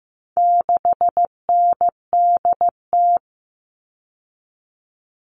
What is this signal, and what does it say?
Morse code, heard as a keyed tone, about 15 wpm, 700 Hz. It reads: 6NDT
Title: All Cut Numbers - 4-Digit 15wpm